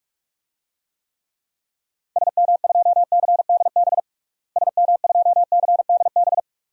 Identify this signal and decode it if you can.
Morse code, heard as a keyed tone, about 46 wpm, 700 Hz. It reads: SM2CDB SM2CDB
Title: DXCC-100 Call Signs - 6 Characters - Repeated 2x - 45wpm